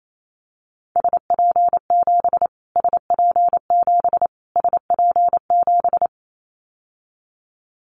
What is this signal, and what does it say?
Morse code, heard as a keyed tone, about 28 wpm, 700 Hz. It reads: SP7 SP7 SP7
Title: Callsign Prefixes for Top 25 DXCC - Repeated 3x - 28wpm